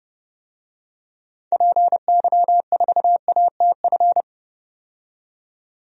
Morse code, heard as a keyed tone, about 30 wpm, 700 Hz. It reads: PY4ATF
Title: DXCC-100 Call Signs - 6 Characters - 30wpm